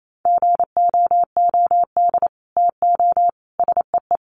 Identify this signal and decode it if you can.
Morse code, heard as a keyed tone, about 28 wpm, 700 Hz. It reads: GOOD TO SEE
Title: Sets of 5 Words Encore2 28wpm